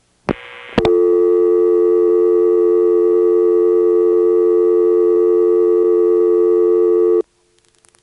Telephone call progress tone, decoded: dial tone